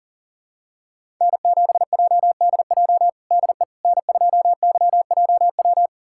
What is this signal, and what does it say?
Morse code, heard as a keyed tone, about 40 wpm, 700 Hz. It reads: N7JDJ DE N2YJW
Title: USA Callsign DE Callsign 40wpm